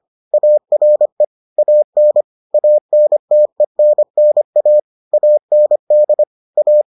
Morse code, heard as a keyed tone, about 25 wpm, 600 Hz. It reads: ARE AN ANTENNA AND A